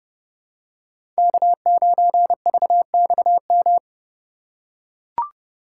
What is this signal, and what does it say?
Morse code, heard as a keyed tone, about 30 wpm, 700 Hz. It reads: K9VXM
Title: USA Call Signs - 5 Characters - 3x Character Spacing - 30wpm